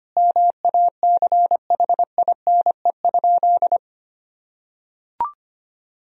Morse code, heard as a keyed tone, about 25 wpm, 700 Hz. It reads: MACHINE?
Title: Sentences from Top 1000 Words - Review Part 2 - 25wpm